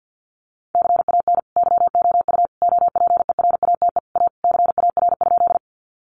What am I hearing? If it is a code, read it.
Morse code, heard as a keyed tone, about 50 wpm, 700 Hz. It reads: CAN YOU OPERATE A CAR?
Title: Sentences from Top 1500 Words 50wpm